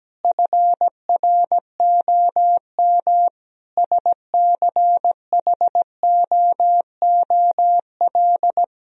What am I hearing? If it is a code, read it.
Morse code, heard as a keyed tone, about 17 wpm, 700 Hz. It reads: FROM SCHOOL
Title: Sentences from Top 1000 Words - Review Part 4 - 17wpm